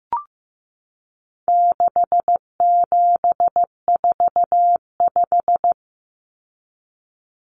Morse code, heard as a keyed tone, about 15 wpm, 700 Hz. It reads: 6745
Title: Numbers - 4-Digit 15wpm